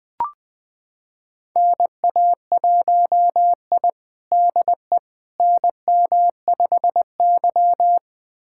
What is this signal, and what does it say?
Morse code, heard as a keyed tone, about 20 wpm, 700 Hz. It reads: NA1I DE NM5Y
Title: USA Callsign DE Callsign - 4 Characters - 20wpm